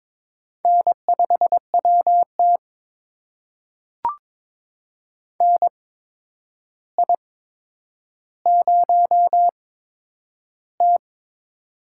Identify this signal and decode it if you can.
Morse code, heard as a keyed tone, about 22 wpm, spaced as for 9 wpm, 700 Hz. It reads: N5WT NI0T
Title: USA Call Signs - 4 Characters - 8x Character Spacing - 22wpm